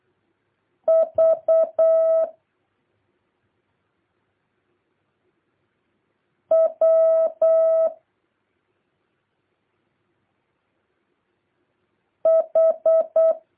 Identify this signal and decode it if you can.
Morse code, heard as a keyed tone, about 8 wpm, 650 Hz. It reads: V W H
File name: jh8cbh / 2023cw